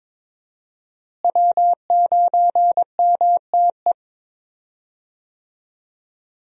Morse code, heard as a keyed tone, about 22 wpm, 700 Hz. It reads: W9MTE
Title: USA Call Signs - 5 Characters - Repeated 2x - 22wpm